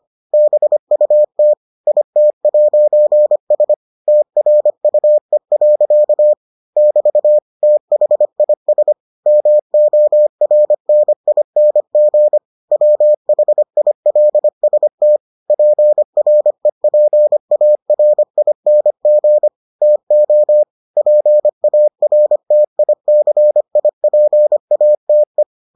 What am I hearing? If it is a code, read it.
Morse code, heard as a keyed tone, about 25 wpm, 600 Hz. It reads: BUT IT'S TRUE. <BT> THIS MORNING WHILST PREPARING TO PARTICIPATE